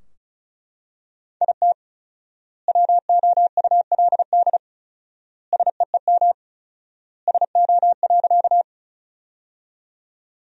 Morse code, heard as a keyed tone, about 36 wpm, 700 Hz. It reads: IT WOULD SEEM SO.
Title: 4x Word-Spacing - Sentences from Top 300 Words - 35wpm